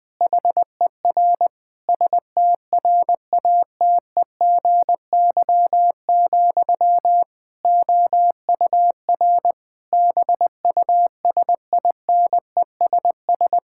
Morse code, heard as a keyed tone, about 20 wpm, 700 Hz. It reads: HER STRATEGY, OUR BUSINESS
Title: Sentences from Top 1000 Words - Review Part 5 - 20wpm